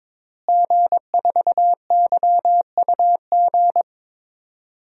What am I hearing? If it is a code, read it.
Morse code, heard as a keyed tone, about 22 wpm, 700 Hz. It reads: G4YUG
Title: DXCC-100 Call Signs - 5 Characters - Repeated 3x - 22wpm